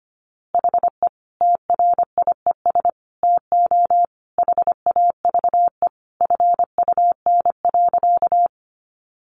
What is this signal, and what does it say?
Morse code, heard as a keyed tone, about 25 wpm, 700 Hz. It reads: HE TRIES TO HAVE FUN.
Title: Easy Phrases 25wpm